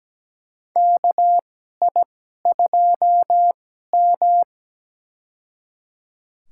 Morse code, heard as a keyed tone, about 17 wpm, spaced as for 12 wpm, 700 Hz. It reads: KI2M
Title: USA Call Signs - 2x Character Spacing - 17wpm